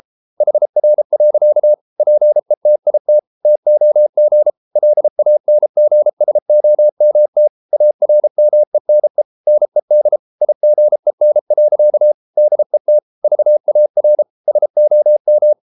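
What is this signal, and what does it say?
Morse code, heard as a keyed tone, about 33 wpm, 600 Hz. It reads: HR. PETIT TOG LANGSOMT ARMENE NED IGEN. DET VAR SOM
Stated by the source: Morse code at 33 wpm, QRQ.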